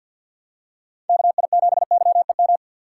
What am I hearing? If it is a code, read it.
Morse code, heard as a keyed tone, about 47 wpm, 700 Hz. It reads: KI7YEM
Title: USA Call Signs - 6 Characters - 4x Character Spacing - 50wpm